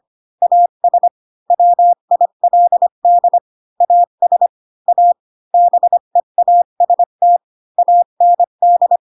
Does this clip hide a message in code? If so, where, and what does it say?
Yes: Morse code from 0.4 to 9.0 s, heard as a keyed tone, about 25 wpm, 700 Hz. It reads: AS WILD AS A BEAST AND